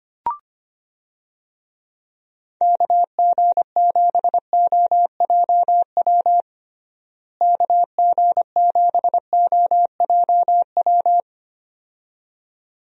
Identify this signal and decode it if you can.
Morse code, heard as a keyed tone, about 25 wpm, 700 Hz. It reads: KG7OJW KG7OJW
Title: USA Call Signs - 6 Characters - Repeated 2x - 25wpm